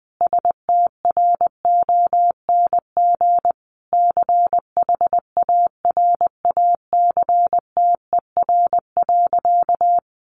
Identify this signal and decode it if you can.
Morse code, heard as a keyed tone, about 19 wpm, 700 Hz. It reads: STRONG CHARACTER.
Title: Sentences from Top 1200 Words 20wpm